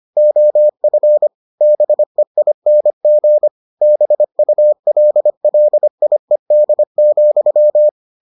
Morse code, heard as a keyed tone, about 25 wpm, 600 Hz. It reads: OF BEING BULLIED,